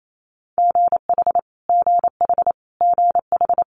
Morse code, heard as a keyed tone, about 28 wpm, 700 Hz. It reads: GH GH GH
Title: Multi-letter Phonemes Repeated 3x 28wpm